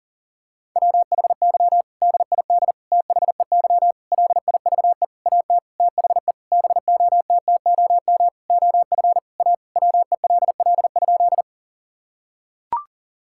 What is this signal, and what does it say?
Morse code, heard as a keyed tone, about 40 wpm, 700 Hz. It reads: WHY DID THEY LIVE AT THE BOTTOM OF A WELL?
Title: Sentences from Top 1000 Words - Review Part 6 - 40wpm